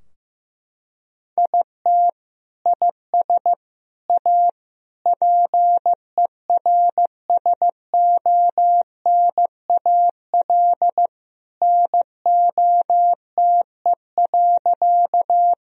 Morse code, heard as a keyed tone, about 15 wpm, 700 Hz. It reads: IT IS A PERSONAL NOTE.
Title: Sentences from Top 1000 Words - Review Part 4 - 15wpm